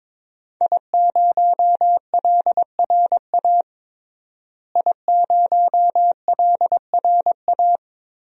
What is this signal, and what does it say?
Morse code, heard as a keyed tone, about 22 wpm, 700 Hz. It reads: I0LRA I0LRA
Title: DXCC-100 Call Signs - 5 Characters - Repeated 3x - 22wpm